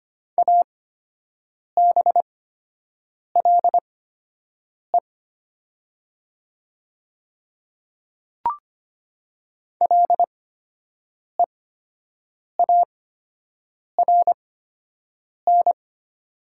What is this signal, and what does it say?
Morse code, heard as a keyed tone, about 25 wpm, spaced as for 7 wpm, 700 Hz. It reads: ABLE LEARN
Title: 8x Character Spacing - Top 500 Words - 25wpm